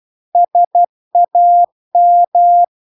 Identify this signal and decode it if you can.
Morse code, heard as a keyed tone, about 12 wpm, 700 Hz. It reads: SAM